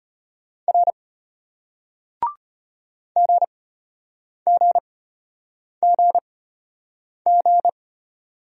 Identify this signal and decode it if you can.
Morse code, heard as a keyed tone, about 35 wpm, spaced as for 8 wpm, 700 Hz. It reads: R GGGG